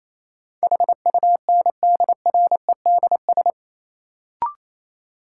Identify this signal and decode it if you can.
Morse code, heard as a keyed tone, about 28 wpm, 700 Hz. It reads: HUNDREDS
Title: Sets of 4 Words - New Word Drill - 28wpm